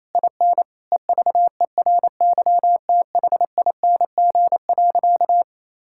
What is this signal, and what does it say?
Morse code, heard as a keyed tone, about 28 wpm, 700 Hz. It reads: IN EVERYTHING.
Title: Sentences from Top 1600 Words 28wpm